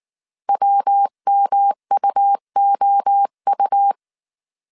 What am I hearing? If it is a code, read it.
Morse code, heard as a keyed tone, about 19 wpm, 800 Hz. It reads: WMUOU
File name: CW / DET / 1 3 1500C S B 18wpm 1916